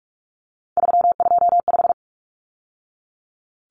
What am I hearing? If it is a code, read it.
Morse code, heard as a keyed tone, about 44 wpm, 700 Hz. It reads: 325
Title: Numbers - 3-Digit 45wpm